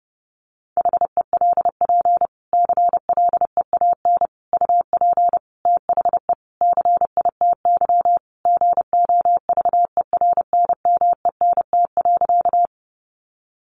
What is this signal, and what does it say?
Morse code, heard as a keyed tone, about 30 wpm, 700 Hz. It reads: HELP CLEAN UP THE CITY GOVERNMENT.
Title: Sentences from Top 1100 Words 30wpm